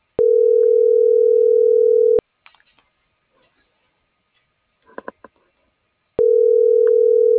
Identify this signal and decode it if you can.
Ringback tone (North America), 440 + 480 Hz.